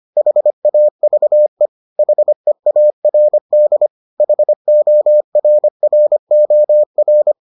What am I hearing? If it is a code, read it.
Morse code, heard as a keyed tone, about 25 wpm, 600 Hz. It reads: HAVE HEARD HORROR